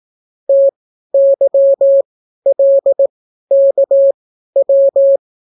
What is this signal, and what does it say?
Morse code, heard as a keyed tone, about 18 wpm, spaced as for 12 wpm, 550 Hz. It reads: TYLKW